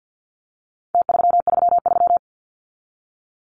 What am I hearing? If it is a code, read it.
Morse code, heard as a keyed tone, about 50 wpm, 700 Hz. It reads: T333
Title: Numbers - Contest Style - 4-Digit 50wpm